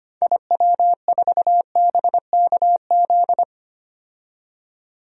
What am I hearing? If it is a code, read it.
Morse code, heard as a keyed tone, about 25 wpm, 700 Hz. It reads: IW4BKZ
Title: DXCC-100 Call Signs - 6 Characters - 25wpm